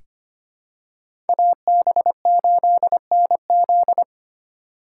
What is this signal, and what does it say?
Morse code, heard as a keyed tone, about 25 wpm, 700 Hz. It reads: AB8NZ
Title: USA Call Signs - 5 Characters - 3x Character Spacing - 25wpm